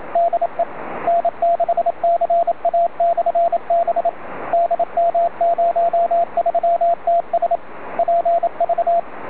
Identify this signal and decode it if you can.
Morse code, heard as a keyed tone, about 27 wpm, 680 Hz, with radio noise. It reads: DE N6CA/B DM03TS PV